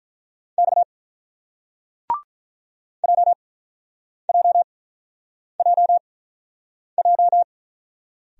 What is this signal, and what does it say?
Morse code, heard as a keyed tone, about 42 wpm, spaced as for 8 wpm, 700 Hz. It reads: X JJJJ
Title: Speed-Racing Single Letters 35wpm